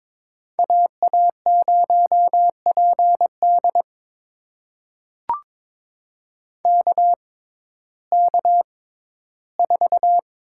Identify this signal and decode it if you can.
Morse code, heard as a keyed tone, about 22 wpm, spaced as for 14 wpm, 700 Hz. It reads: AA0PD KK4
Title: USA Call Signs - 6x Character Spacing - 22wpm